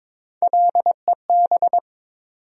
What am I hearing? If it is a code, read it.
Morse code, heard as a keyed tone, about 22 wpm, 700 Hz. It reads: LEB
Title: USA Call Sign Postfixes - 3 Letters 22wpm